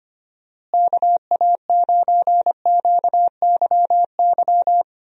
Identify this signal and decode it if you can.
Morse code, heard as a keyed tone, about 25 wpm, 700 Hz. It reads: KA9QYY
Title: USA Call Signs - 6 Characters - Repeated 2x - 25wpm